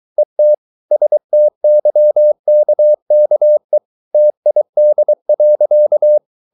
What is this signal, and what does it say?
Morse code, heard as a keyed tone, about 23 wpm, 600 Hz. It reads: ET STYKKE TID.
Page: Ferie 23 wpm